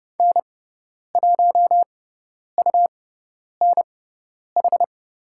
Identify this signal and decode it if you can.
Morse code, heard as a keyed tone, about 30 wpm, spaced as for 10 wpm, 700 Hz. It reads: N1UNH
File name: Lesson 17g - Letter u - Call Signs 30wpm